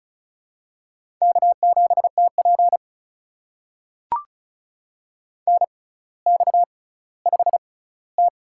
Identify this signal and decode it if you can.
Morse code, heard as a keyed tone, about 36 wpm, spaced as for 18 wpm, 700 Hz. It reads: K7TP NX5T
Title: USA Call Signs - 4 Characters - 6x Character Spacing - 35wpm